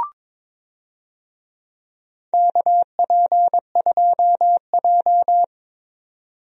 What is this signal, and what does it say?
Morse code, heard as a keyed tone, about 22 wpm, 700 Hz. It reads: KP2J
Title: DXCC-100 Call Signs - 4 Characters - 22wpm